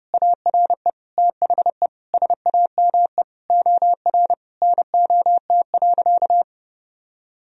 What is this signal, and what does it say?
Morse code, heard as a keyed tone, about 32 wpm, 700 Hz. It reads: ARE THE SAME OR NOT.
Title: Sentences from Top 1000 Words - Review Part 4 - 30wpm